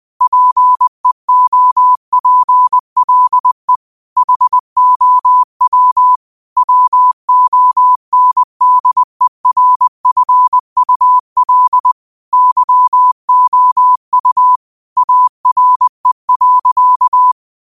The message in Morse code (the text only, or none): PEOPLE HOW WONDERFUL YOU ARE.